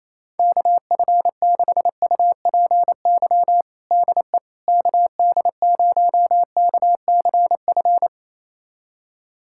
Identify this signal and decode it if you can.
Morse code, heard as a keyed tone, about 28 wpm, 700 Hz. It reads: KF6UPY DE KD0KCF
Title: USA Callsign DE Callsign - 6 Characters - 28wpm